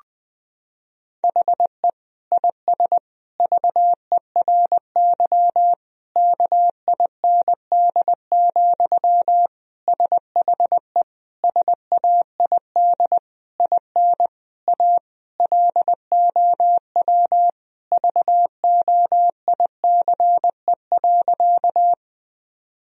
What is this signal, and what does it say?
Morse code, heard as a keyed tone, about 20 wpm, 700 Hz. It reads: HE IS VERY KIND, SHE SAID IN A LOW VOICE.
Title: Sentences from Top 1000 Words - Review Part 1 - 20wpm